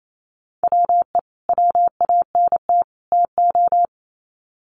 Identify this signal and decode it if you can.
Morse code, heard as a keyed tone, about 28 wpm, 700 Hz. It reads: WE WANT TO
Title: Sets of 3 Words 28wpm